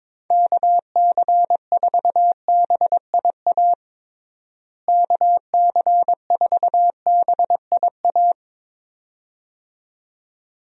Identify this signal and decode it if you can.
Morse code, heard as a keyed tone, about 21 wpm, 700 Hz. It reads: KC4BIA KC4BIA
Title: USA Call Signs - 6 Characters - Repeated 2x - 22wpm